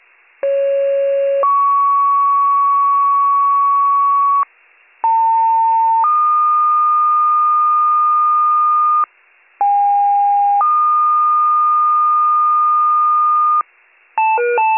Two-tone page, heard as a one-second tone, then a three-second tone, about 550 then 1120 Hz, about 900 then 1220 Hz, about 810 then 1220 Hz. It is heard from 0.4 to 13.6 s.